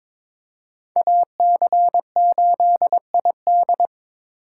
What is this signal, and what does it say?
Morse code, heard as a keyed tone, about 26 wpm, 700 Hz. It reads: AC8ID
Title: Speed Racing - USA 2x2 Call Signs 22wpm